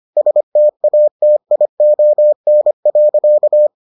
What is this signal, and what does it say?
Morse code, heard as a keyed tone, about 25 wpm, 600 Hz. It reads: STATION.